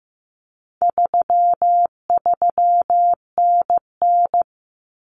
Morse code, heard as a keyed tone, about 14 wpm, 700 Hz. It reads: 33NN
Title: Common Cut Numbers - 4-Digit 15wpm